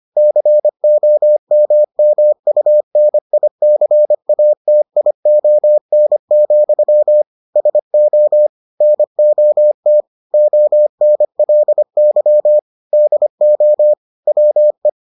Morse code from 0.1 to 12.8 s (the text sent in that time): COMMUNICATION, SO NOT ONLY